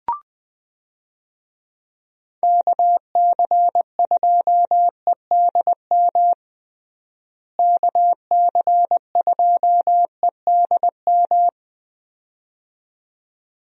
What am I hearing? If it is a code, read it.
Morse code, heard as a keyed tone, about 20 wpm, 700 Hz. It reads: KC2EDM KC2EDM
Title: USA Call Signs - 6 Characters - Repeated 2x - 20wpm